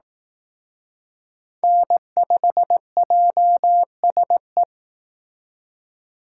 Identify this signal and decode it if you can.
Morse code, heard as a keyed tone, about 18 wpm, 700 Hz. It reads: N5JSE
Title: Speed Racing - USA 1x3 Call Signs 15wpm